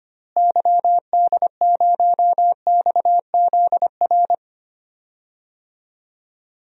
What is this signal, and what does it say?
Morse code, heard as a keyed tone, about 25 wpm, 700 Hz. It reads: YD0XZR
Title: DXCC-100 Call Signs - 6 Characters - Repeated 2x - 25wpm